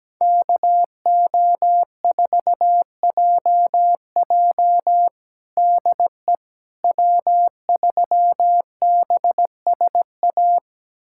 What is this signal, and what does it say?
Morse code, heard as a keyed tone, about 17 wpm, 700 Hz. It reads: KO4JJ DE W3BSA
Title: USA Callsign DE Callsign - 5 Characters - 17wpm